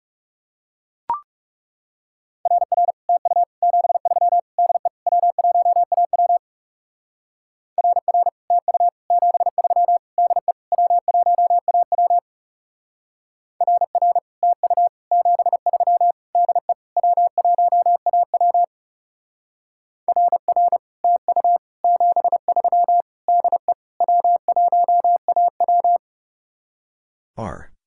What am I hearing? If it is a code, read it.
Morse code, heard as a keyed tone, about 38 wpm, 700 Hz. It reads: RR TU 73 DE W1AW RR TU 73 DE W1AW RR TU 73 DE W1AW RR TU 73 DE W1AW